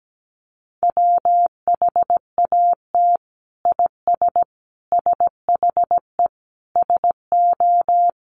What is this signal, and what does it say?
Morse code, heard as a keyed tone, about 17 wpm, 700 Hz. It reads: WHAT IS SHE SO
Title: Sentences from Top 800 Words 17wpm